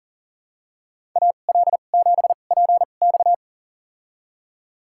Morse code, heard as a keyed tone, about 40 wpm, spaced as for 29 wpm, 700 Hz. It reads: AL7PX
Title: USA Call Signs - 5 Characters - 2x Character Spacing - 40wpm